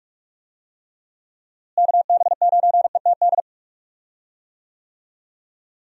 Morse code, heard as a keyed tone, about 46 wpm, 700 Hz. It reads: KB9ETD
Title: USA Call Signs - 6 Characters - Repeated 3x - 45wpm